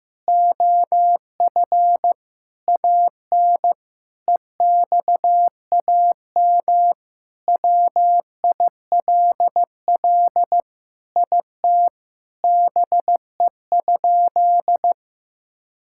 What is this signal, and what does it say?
Morse code, heard as a keyed tone, about 15 wpm, 700 Hz. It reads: OF AN EXAM WILL IT BE?
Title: Sentences from Top 1000 Words - Review Part 5 - 15wpm